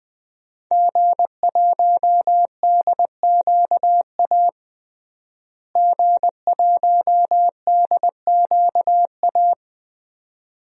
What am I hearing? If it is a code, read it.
Morse code, heard as a keyed tone, about 21 wpm, 700 Hz. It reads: G1DQA G1DQA
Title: DXCC-100 Call Signs - 5 Characters - Repeated 2x - 20wpm